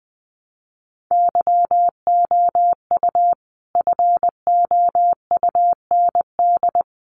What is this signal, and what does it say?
Morse code, heard as a keyed tone, about 20 wpm, 700 Hz. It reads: YOU FOUND
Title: Sentences from Top 300 Words 20wpm